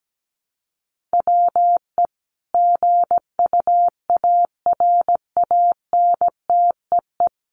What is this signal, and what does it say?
Morse code, heard as a keyed tone, about 17 wpm, 700 Hz. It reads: WE GUARANTEE
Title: A Sound Of Thunder 17wpm